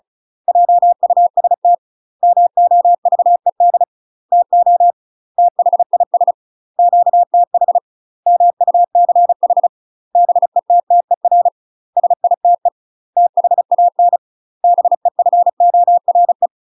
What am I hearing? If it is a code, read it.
Morse code, heard as a keyed tone, about 34 wpm, 700 Hz. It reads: JUST MOVED TO THIS QTH MUCH BETTER SITE THAN BEFORE
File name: UK MASTER QSOs TRAINING / 35WPM / UK MASTER QSOs 0030